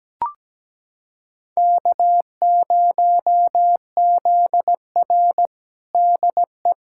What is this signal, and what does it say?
Morse code, heard as a keyed tone, about 17 wpm, 700 Hz. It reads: K0ZR DE
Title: Practice QSO - Exchange 4 - Ending - 17wpm